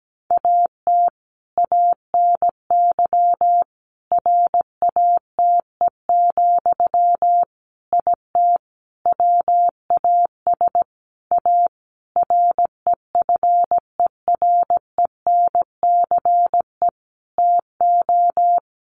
Morse code, heard as a keyed tone, about 17 wpm, 700 Hz. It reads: AT ANY RATE, IT WAS A REFERENCE TO
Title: Sentences from Top 1200 Words 17wpm